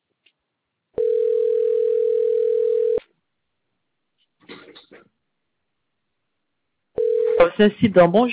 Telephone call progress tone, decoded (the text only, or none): ringback tone